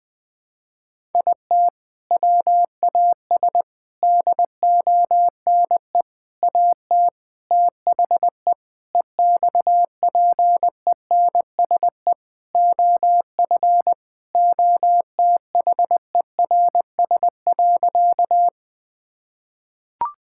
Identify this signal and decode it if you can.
Morse code, heard as a keyed tone, about 20 wpm, 700 Hz. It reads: IT WAS DONE AT THE EXPENSE OF OTHERS.
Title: Sentences with 4-Word Phrases - 20wpm